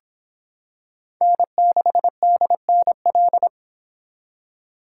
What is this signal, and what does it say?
Morse code, heard as a keyed tone, about 25 wpm, 700 Hz. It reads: N6DNL